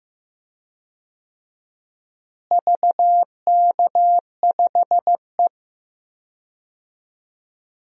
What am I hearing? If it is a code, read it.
Morse code, heard as a keyed tone, about 15 wpm, 700 Hz. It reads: VK5E
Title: DXCC-100 Call Signs - 4 Characters - Repeated 2x - 15wpm